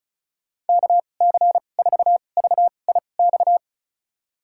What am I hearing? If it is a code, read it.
Morse code, heard as a keyed tone, about 34 wpm, spaced as for 25 wpm, 700 Hz. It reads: KC4VIX
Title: USA Call Signs - 6 Characters - 2x Character Spacing - 35wpm